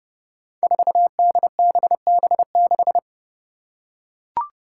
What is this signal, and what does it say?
Morse code, heard as a keyed tone, about 30 wpm, 700 Hz. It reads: 4DBB6
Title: Five Challenging Letter-Number Combinations 30wpm